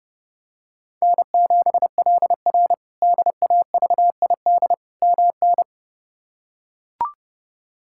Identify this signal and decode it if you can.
Morse code, heard as a keyed tone, about 32 wpm, 700 Hz. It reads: N7LR DAVID MN
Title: K1USN SST QSO Practice - 30wpm